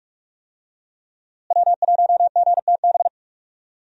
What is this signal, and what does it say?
Morse code, heard as a keyed tone, about 47 wpm, 700 Hz. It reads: W1GTB
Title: USA Call Signs - 5 Characters - Repeated 2x - 45wpm